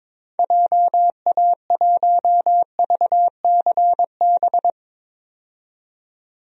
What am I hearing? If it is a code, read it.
Morse code, heard as a keyed tone, about 22 wpm, 700 Hz. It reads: JA1VCB
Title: DXCC-100 Call Signs - 6 Characters - 22wpm